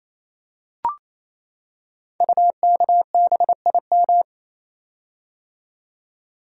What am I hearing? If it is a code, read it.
Morse code, heard as a keyed tone, about 27 wpm, 700 Hz. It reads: UKBIM
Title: Code Groups - 5 Letters - 28wpm